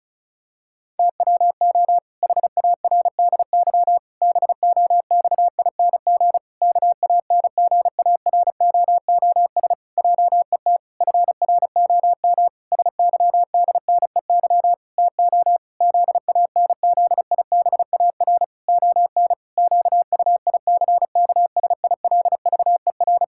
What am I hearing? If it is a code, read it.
Morse code, heard as a keyed tone, about 36 wpm, 700 Hz. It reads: TWO HARDY BOXING KANGAROOS JET FROM SYDNEY TO ZANZIBAR ON QUICKSILVER
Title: Pangram Challenge v2 - 35wpm